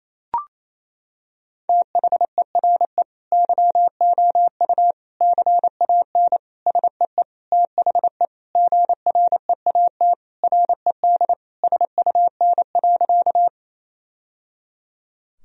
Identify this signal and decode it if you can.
Morse code, heard as a keyed tone, about 28 wpm, 700 Hz. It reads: THERE YOU CAN SEE THE GREAT RED SUN.
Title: Sentences from Top 1000 Words - Review Part 2 - 28wpm